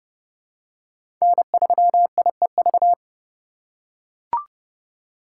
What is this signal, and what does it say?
Morse code, heard as a keyed tone, about 32 wpm, 700 Hz. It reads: N3IEV
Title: USA Call Signs - 5 Characters - 3x Character Spacing - 30wpm